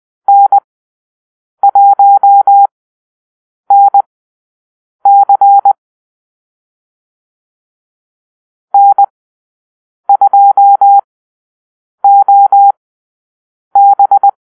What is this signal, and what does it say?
Morse code, heard as a keyed tone, about 20 wpm, spaced as for 7 wpm, 800 Hz. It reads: N1NC N2OB
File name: arrl / Call-Sign Drill-7